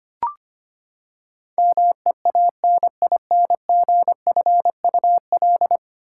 Morse code, heard as a keyed tone, about 25 wpm, 700 Hz. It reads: MEANINGFUL
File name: Ten Letter Words - 25wpm